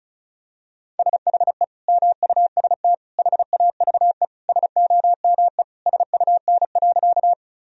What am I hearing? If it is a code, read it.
Morse code, heard as a keyed tone, about 36 wpm, 700 Hz. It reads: SHE MUST HAVE SOME SUN.
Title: Sentences from Top 1000 Words - Review Part 1 - 35wpm